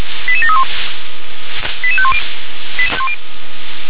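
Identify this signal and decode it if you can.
Five-tone selective call (ZVEI), heard as quick runs of five tones, tones of about 70 ms. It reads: 88731 88510
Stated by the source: Five-tone ZVEI selective call received by SDR.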